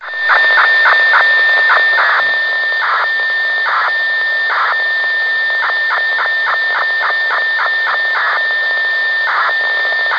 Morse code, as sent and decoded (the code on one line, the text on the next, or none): none